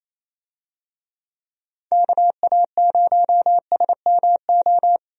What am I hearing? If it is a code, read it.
Morse code, heard as a keyed tone, about 28 wpm, 700 Hz. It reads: KA0SMO
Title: USA Call Signs - 6 Characters - 28wpm